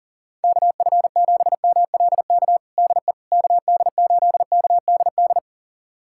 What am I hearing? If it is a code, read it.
Morse code, heard as a keyed tone, about 40 wpm, 700 Hz. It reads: KF7MLK DE KD8KDD
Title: USA Callsign DE Callsign 40wpm